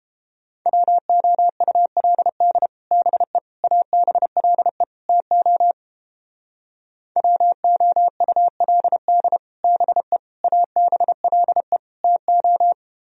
Morse code, heard as a keyed tone, about 31 wpm, 700 Hz. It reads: WOULD BE ABLE TO WOULD BE ABLE TO